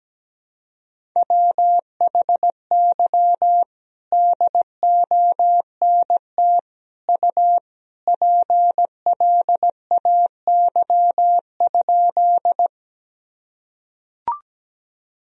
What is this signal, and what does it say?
Morse code, heard as a keyed tone, about 17 wpm, 700 Hz. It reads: WHY DONT U PLAY?
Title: Sentences from Top 300 Words and Abbreviations 17wpm